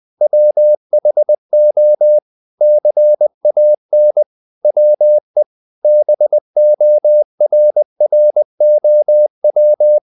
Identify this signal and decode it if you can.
Morse code, heard as a keyed tone, about 20 wpm, 600 Hz. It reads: WHO CAN WE BORROW
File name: the old man and the sea / the old man and the sea part 1 20wpm